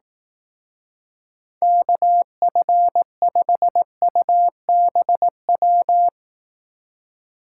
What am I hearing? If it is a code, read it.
Morse code, heard as a keyed tone, about 18 wpm, 700 Hz. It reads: KF5UBW
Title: Speed Racing - USA 2x3 Call Signs 15wpm